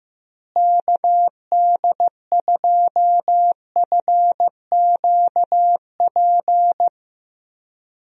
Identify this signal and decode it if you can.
Morse code, heard as a keyed tone, about 15 wpm, 700 Hz. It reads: KD2FQP
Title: USA Call Signs - 6 Characters - Repeated 3x - 15wpm